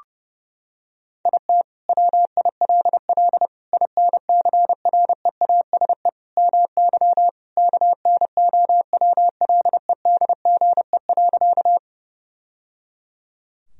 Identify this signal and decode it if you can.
Morse code, heard as a keyed tone, about 30 wpm, 700 Hz. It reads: IT WILL INCREASE MY KNOWLEDGE.
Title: Sentences from Top 1000 Words - Review Part 4 - 30wpm